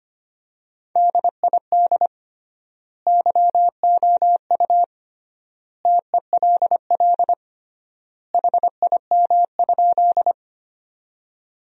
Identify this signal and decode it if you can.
Morse code, heard as a keyed tone, about 25 wpm, 700 Hz. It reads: DID YOU TELL HIM?
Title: 3x Word-Spacing - Sentences from Top 200 Words - 25wpm